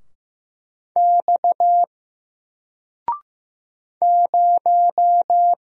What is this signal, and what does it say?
Morse code, heard as a keyed tone, about 15 wpm, 700 Hz. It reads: X 0
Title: Single Letter-Number 15wpm